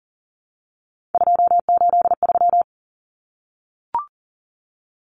Morse code, heard as a keyed tone, about 40 wpm, 700 Hz. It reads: 283
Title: Common Cut Numbers - 3-Digit 40wpm